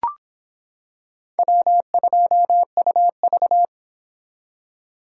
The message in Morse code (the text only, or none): W2UV